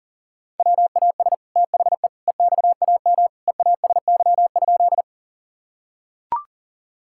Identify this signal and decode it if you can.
Morse code, heard as a keyed tone, about 40 wpm, 700 Hz. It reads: WAS THE EXAM EASY?
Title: Sentences from Top 1000 Words - Review Part 5 - 40wpm